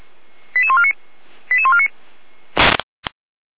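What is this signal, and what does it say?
Five-tone selective call (ZVEI), heard as quick runs of five tones, tones of about 70 ms. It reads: 88149 88149
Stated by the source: ZVEI five-tone selective call received by SDR, archived by Feuerwehr Orlamünde (fire brigade).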